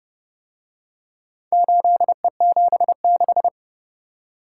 Morse code, heard as a keyed tone, about 29 wpm, 700 Hz. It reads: 8E76